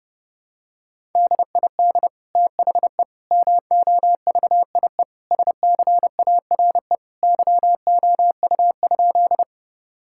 Morse code, heard as a keyed tone, about 32 wpm, 700 Hz. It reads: DID THE MOVIE SCARE YOU?
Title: Sentences from Top 1000 Words - Review Part 5 - 30wpm